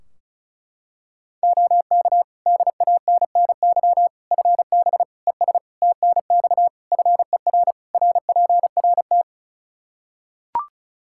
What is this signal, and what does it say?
Morse code, heard as a keyed tone, about 34 wpm, 700 Hz. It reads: OK DANNY FB ES TNX FER RPRT
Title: Practice QSO - Exchange 2 - Info - 35wpm